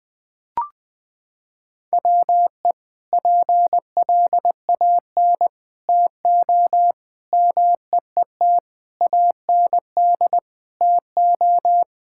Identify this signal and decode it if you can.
Morse code, heard as a keyed tone, about 20 wpm, 700 Hz. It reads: WE PLAN TO MEET AND TO
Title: Sentences with 2-Word Phrases - 20wpm